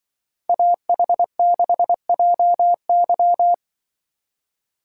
Morse code, heard as a keyed tone, about 24 wpm, 700 Hz. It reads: AH6JY
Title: Speed Racing - USA Call Signs 20wpm